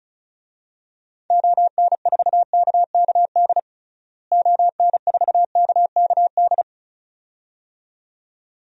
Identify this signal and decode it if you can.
Morse code, heard as a keyed tone, about 36 wpm, 700 Hz. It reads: ON4KKD ON4KKD
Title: DXCC-100 Call Signs - 6 Characters - Repeated 2x - 35wpm